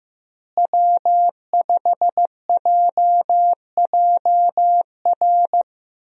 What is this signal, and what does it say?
Morse code, heard as a keyed tone, about 15 wpm, 700 Hz. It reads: W5JJR
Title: USA Call Signs - 5 Characters - Repeated 3x - 15wpm